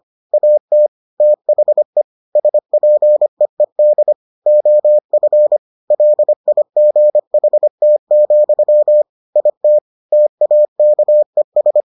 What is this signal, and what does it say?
Morse code, heard as a keyed tone, about 25 wpm, 600 Hz. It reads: AT THE SPEED OF LIGHT, IT TAKES